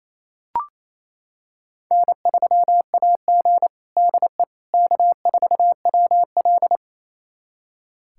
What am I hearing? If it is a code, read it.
Morse code, heard as a keyed tone, about 28 wpm, 700 Hz. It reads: N3AG DE K4WL
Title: USA Callsign DE Callsign - 4 Characters - 28wpm